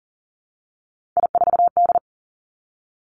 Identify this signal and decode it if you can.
Morse code, heard as a keyed tone, about 40 wpm, 700 Hz. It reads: I4D
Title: Three Challenging Letter-Number Combinations 40wpm